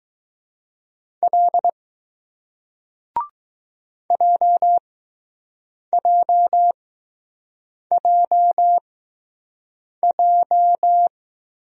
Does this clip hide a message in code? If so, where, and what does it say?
Yes: Morse code from 1.2 to 11.1 s, heard as a keyed tone, about 20 wpm, spaced as for 6 wpm, 700 Hz. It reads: L JJJJ